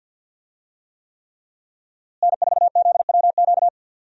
Morse code, heard as a keyed tone, about 50 wpm, 700 Hz. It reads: N4ZWQ
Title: USA Call Signs - 5 Characters - Repeated 3x - 50wpm